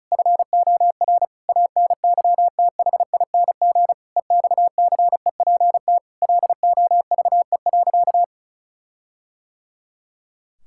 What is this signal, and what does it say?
Morse code, heard as a keyed tone, about 34 wpm, 700 Hz. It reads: FOR ANYTHING EXCEPT LOVE.
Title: Sentences from Top 1600 Words 35wpm